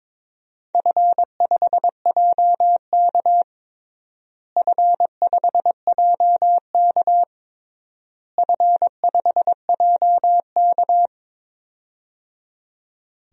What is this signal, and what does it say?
Morse code, heard as a keyed tone, about 22 wpm, 700 Hz. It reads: F5JK F5JK F5JK
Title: DXCC-100 Call Signs - 4 Characters - Repeated 3x - 22wpm